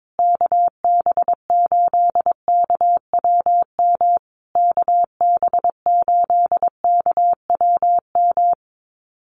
Morse code, heard as a keyed tone, about 22 wpm, 700 Hz. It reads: KB8KWM KB8KWM
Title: USA Call Signs Repeated 3x 22wpm